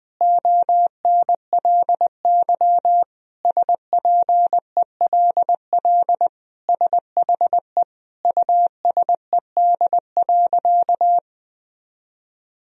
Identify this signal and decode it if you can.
Morse code, heard as a keyed tone, about 21 wpm, 700 Hz. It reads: ONLY SPELL SHE USED.
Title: Sentences from Top 1000 Words - Review Part 1 - 20wpm